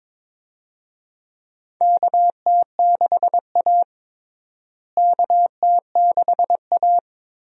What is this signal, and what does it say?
Morse code, heard as a keyed tone, about 21 wpm, 700 Hz. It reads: KT6A KT6A
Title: USA Call Signs - 4 Characters - Repeated 2x - 22wpm